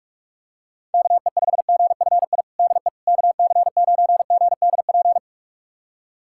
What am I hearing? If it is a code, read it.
Morse code, heard as a keyed tone, about 41 wpm, 700 Hz. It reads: KE5GFI DE KC9GDP
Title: USA Callsign DE Callsign 45wpm